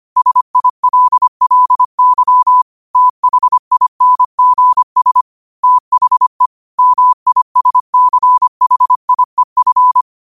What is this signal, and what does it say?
Morse code, heard as a keyed tone, about 25 wpm, 1000 Hz. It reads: SILLY THINGS THE MISCHIEF
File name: qotdfiles / 25wpm